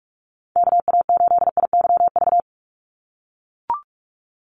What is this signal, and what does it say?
Morse code, heard as a keyed tone, about 44 wpm, 700 Hz. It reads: KA8IYV
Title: USA Call Signs 45wpm